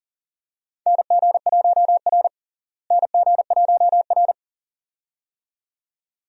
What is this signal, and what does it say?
Morse code, heard as a keyed tone, about 40 wpm, 700 Hz. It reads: NG1R NG1R
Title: USA Call Signs - 4 Characters - Repeated 2x - 40wpm